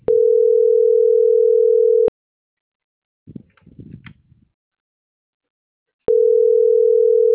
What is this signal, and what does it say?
Ringback tone (North America), 440 + 480 Hz.